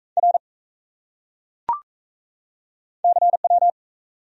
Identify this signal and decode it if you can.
Morse code, heard as a keyed tone, about 35 wpm, 700 Hz. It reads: R CW